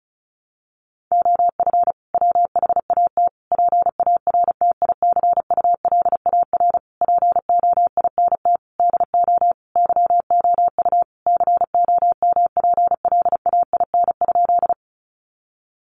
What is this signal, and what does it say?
Morse code, heard as a keyed tone, about 34 wpm, 700 Hz. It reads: OF WHAT PARTICULAR POINT DO YOU COMPLAIN?
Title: Sentences from Top 1300 Words 35wpm